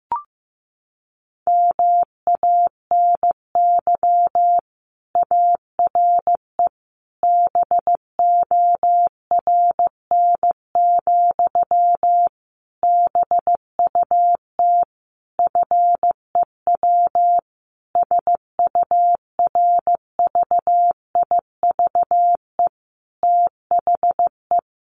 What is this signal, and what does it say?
Morse code, heard as a keyed tone, about 15 wpm, 700 Hz. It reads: MANY ARE BORN, BUT FEW SURVIVE THE
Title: Sentences from Top 1300 Words 15wpm